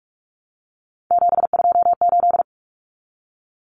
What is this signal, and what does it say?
Morse code, heard as a keyed tone, about 44 wpm, 700 Hz. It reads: 728
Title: Common Cut Numbers - 3-Digit 45wpm